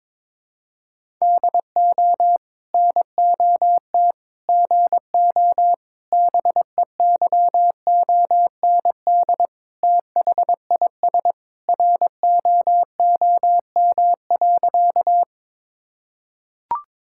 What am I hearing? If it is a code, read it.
Morse code, heard as a keyed tone, about 22 wpm, 700 Hz. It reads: DO NOT GO BEYOND THIS ROOM.
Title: Sentences from Top 1000 Words - Review Part 6 - 22wpm